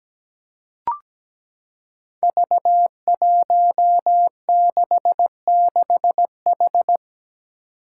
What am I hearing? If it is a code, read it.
Morse code, heard as a keyed tone, about 17 wpm, 700 Hz. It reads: V166H
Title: Five Challenging Letter-Number Combinations 17wpm